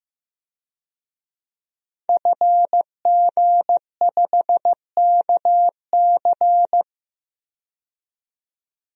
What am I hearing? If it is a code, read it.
Morse code, heard as a keyed tone, about 15 wpm, 700 Hz. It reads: FG5KC
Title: DXCC-100 Call Signs - 5 Characters - 15wpm